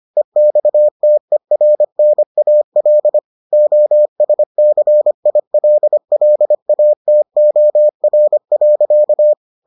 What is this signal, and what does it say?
Morse code, heard as a keyed tone, about 25 wpm, 600 Hz. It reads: EXTERNAL OSCILLATOR.